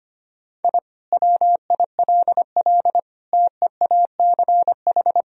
Morse code, heard as a keyed tone, about 25 wpm, 700 Hz. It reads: I WILL TEACH
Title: Sentences from Top 1000 Words - Review Part 2 - 25wpm